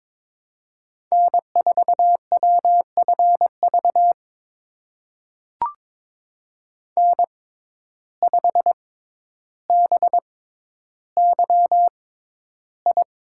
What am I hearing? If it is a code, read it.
Morse code, heard as a keyed tone, about 22 wpm, spaced as for 11 wpm, 700 Hz. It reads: N4WFV N5BYI
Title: USA Call Signs - 5 Characters - 6x Character Spacing - 22wpm